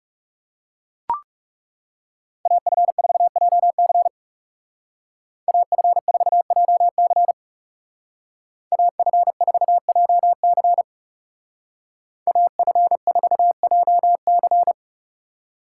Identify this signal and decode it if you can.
Morse code, heard as a keyed tone, about 34 wpm, 700 Hz. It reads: AF4JC AF4JC AF4JC AF4JC
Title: Speed Racing - USA 2x2 Call Signs 30wpm